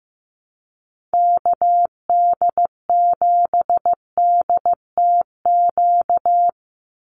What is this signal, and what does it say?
Morse code, heard as a keyed tone, about 15 wpm, 700 Hz. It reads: KD7DTQ
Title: USA Call Signs 15wpm